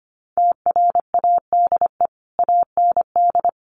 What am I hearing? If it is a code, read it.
Morse code, heard as a keyed tone, about 25 wpm, 700 Hz. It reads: TRADE AND
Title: Sentences from Top 1500 Words 25wpm